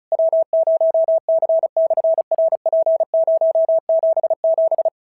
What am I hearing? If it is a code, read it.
Morse code, heard as a keyed tone, about 36 wpm, 650 Hz. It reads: W0C/RP077
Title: SOTA QSO Practice 35wpm